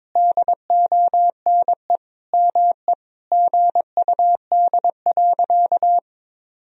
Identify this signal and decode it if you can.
Morse code, heard as a keyed tone, about 22 wpm, 700 Hz. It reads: DONE ME GUD.
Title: Sentences from Top 500 Words and Abbreviations 22wpm